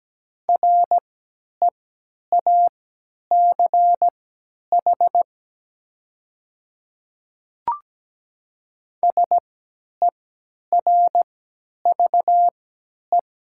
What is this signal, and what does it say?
Morse code, heard as a keyed tone, about 17 wpm, spaced as for 10 wpm, 700 Hz. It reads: REACH SERVE
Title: 3x Character Spacing - Top 500 Words - 17wpm